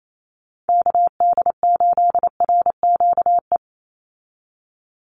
Morse code, heard as a keyed tone, about 28 wpm, 700 Hz. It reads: KD8RQE
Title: Call Signs Repeated 2x - 28wpm